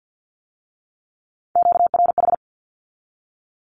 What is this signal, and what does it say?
Morse code, heard as a keyed tone, about 50 wpm, 700 Hz. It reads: QRH
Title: Q-Codes with Spoken Meaning - 50wpm